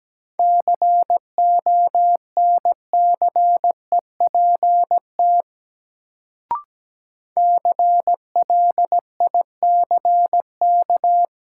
Morse code, heard as a keyed tone, about 17 wpm, 700 Hz. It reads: CONCEPT CLICK
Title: Sentences from Top 1100 Words - New Word Drill - 17wpm